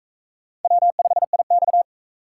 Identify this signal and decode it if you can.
Morse code, heard as a keyed tone, about 41 wpm, 700 Hz. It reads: W5IX